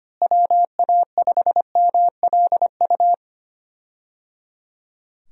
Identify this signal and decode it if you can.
Morse code, heard as a keyed tone, about 25 wpm, 700 Hz. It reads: WA5MLU
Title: USA Call Signs - 6 Characters - Repeated 3x - 25wpm